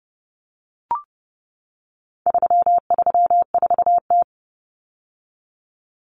Morse code, heard as a keyed tone, about 30 wpm, 700 Hz. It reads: 334T
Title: Common Cut Numbers - 4-Digit 30wpm